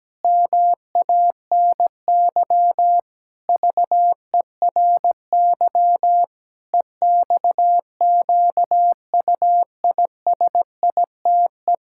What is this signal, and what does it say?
Morse code, heard as a keyed tone, about 17 wpm, 700 Hz. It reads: MANY VERY EXQUISITE
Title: Pangram Challenge v2 - 17wpm